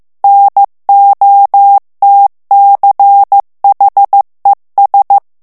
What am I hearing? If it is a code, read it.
Morse code, heard as a keyed tone, about 15 wpm, 800 Hz. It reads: NOTCHES
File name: morse mp3 / ALLEMAND texte clair 1 vitesse 14 mots minute